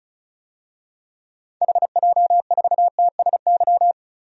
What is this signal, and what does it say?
Morse code, heard as a keyed tone, about 36 wpm, 700 Hz. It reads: HJ4TSY
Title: DXCC-100 Call Signs - 6 Characters - Repeated 3x - 35wpm